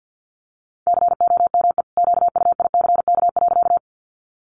Morse code, heard as a keyed tone, about 50 wpm, 700 Hz. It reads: COME QUICK.
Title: Sentences from Top 500 Words 50wpm